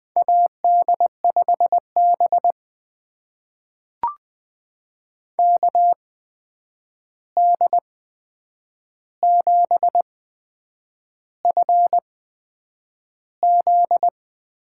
Morse code, heard as a keyed tone, about 20 wpm, spaced as for 8 wpm, 700 Hz. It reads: AD5B KD7FZ
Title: USA Call Signs - 8x Character Spacing - 20wpm